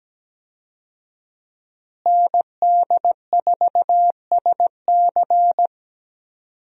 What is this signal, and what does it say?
Morse code, heard as a keyed tone, about 17 wpm, 700 Hz. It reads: ND4SC
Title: USA Call Signs - 5 Characters - Repeated 2x - 17wpm